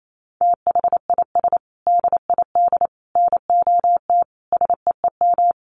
Morse code, heard as a keyed tone, about 28 wpm, 700 Hz. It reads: THIS DID NOT SEEM
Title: Sentences from Top 1400 Words 28wpm